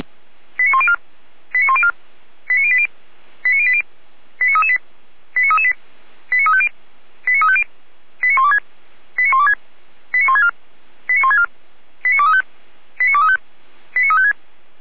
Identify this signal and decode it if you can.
Five-tone selective call (ZVEI), heard as quick runs of five tones, tones of about 70 ms. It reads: 89104 89104 89080 89080 89338 89338 89350 89350 89126 89126 89164 89164 89235 89235 89356